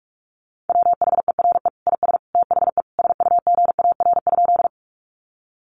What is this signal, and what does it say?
Morse code, heard as a keyed tone, about 44 wpm, 700 Hz. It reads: WHERE IS THE SUGAR?
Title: Sentences from Top 1200 Words 45wpm